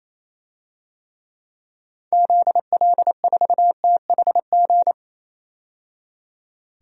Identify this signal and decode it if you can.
Morse code, heard as a keyed tone, about 28 wpm, 700 Hz. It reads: ZL4THG
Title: DXCC-100 Call Signs - 6 Characters - 28wpm